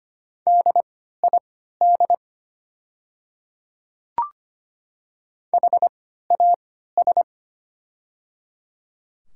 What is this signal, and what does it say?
Morse code, heard as a keyed tone, about 25 wpm, spaced as for 14 wpm, 700 Hz. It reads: DID HAS
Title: 3x Character Spacing - Top 100 Words - 25wpm